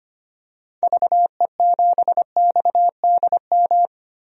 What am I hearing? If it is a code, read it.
Morse code, heard as a keyed tone, about 25 wpm, 700 Hz. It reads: VE7XDM
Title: DXCC-100 Call Signs - 6 Characters - Repeated 3x - 25wpm